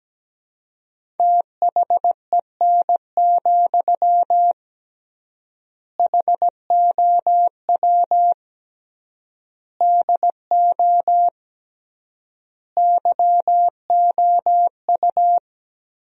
Morse code, heard as a keyed tone, about 17 wpm, 700 Hz. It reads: THEN, HOW DO YOU
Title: 3x Word-Spacing - Sentences from Top 100 Words - 17wpm